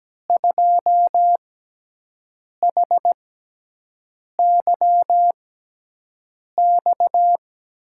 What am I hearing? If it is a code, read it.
Morse code, heard as a keyed tone, about 17 wpm, spaced as for 6 wpm, 700 Hz. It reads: 2HYX
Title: USA Call Signs - 6 Characters - 6x Character Spacing - 17wpm